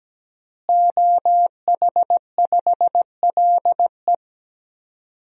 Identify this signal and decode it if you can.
Morse code, heard as a keyed tone, about 17 wpm, 700 Hz. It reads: OH5LE
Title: DXCC-100 Call Signs - 5 Characters - Repeated 2x - 17wpm